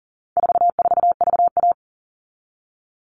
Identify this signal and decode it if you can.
Morse code, heard as a keyed tone, about 40 wpm, 700 Hz. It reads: 44VA
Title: All Cut Numbers - 4-Digit 40wpm